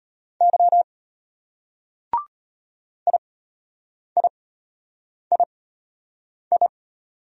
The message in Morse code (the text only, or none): Y IIII